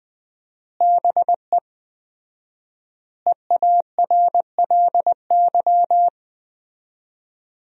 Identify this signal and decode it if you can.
Morse code, heard as a keyed tone, about 20 wpm, 700 Hz. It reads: BE EARLY
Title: 4x Word-Spacing - Sentences from Top 400 Words - 20wpm